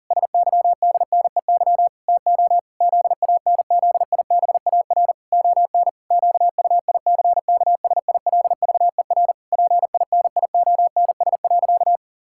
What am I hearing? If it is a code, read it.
Morse code, heard as a keyed tone, about 40 wpm, 700 Hz. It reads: SYDNEY TO ZANZIBAR ON QUICKSILVER PINIONS.
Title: Pangram Challenge v2 - 40wpm